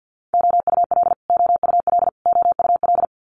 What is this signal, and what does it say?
Morse code, heard as a keyed tone, about 50 wpm, 690 Hz. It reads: OUL OUL OUL
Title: Multi-letter Phonemes Repeated 3x 50wpm